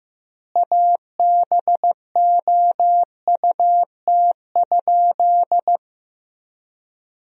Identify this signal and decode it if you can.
Morse code, heard as a keyed tone, about 15 wpm, 700 Hz. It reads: ABOUT?
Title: 3x Word-Spacing - Sentences from Top 100 Words - 15wpm